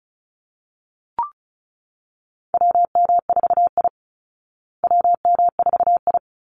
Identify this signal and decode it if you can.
Morse code, heard as a keyed tone, about 34 wpm, 700 Hz. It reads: WM4I WM4I
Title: Call Signs Repeated 3x - 35wpm